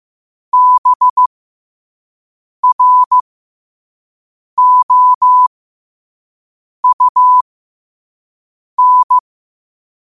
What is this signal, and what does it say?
Morse code, heard as a keyed tone, about 15 wpm, spaced as for 5 wpm, 1000 Hz. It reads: BROUN
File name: qotdfiles / 5wpm